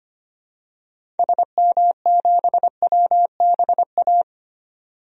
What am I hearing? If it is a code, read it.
Morse code, heard as a keyed tone, about 25 wpm, 700 Hz. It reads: SM7WBA
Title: DXCC-100 Call Signs - 6 Characters - 25wpm